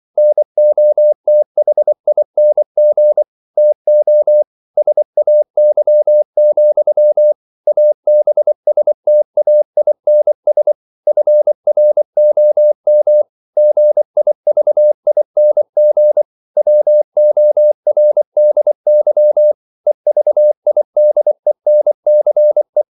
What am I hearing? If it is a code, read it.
Morse code, heard as a keyed tone, about 24 wpm, 600 Hz. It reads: NOTHING TO SAY, ABSTAINS FROM GIVING WORDY EVIDENCE
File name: CW / Practice / EnglishQuotations0224